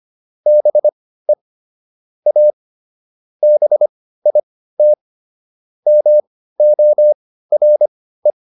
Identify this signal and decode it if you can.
Morse code, heard as a keyed tone, about 25 wpm, spaced as for 15 wpm, 600 Hz. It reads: BE A BIT MORE